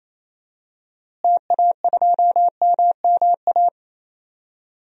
Morse code, heard as a keyed tone, about 28 wpm, 700 Hz. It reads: TA2MMA
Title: DXCC-100 Call Signs - 6 Characters - Repeated 2x - 28wpm